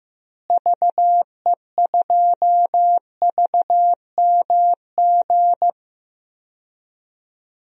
Morse code, heard as a keyed tone, about 15 wpm, 700 Hz. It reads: VE2VMG
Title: DXCC-100 Call Signs - 6 Characters - Repeated 2x - 15wpm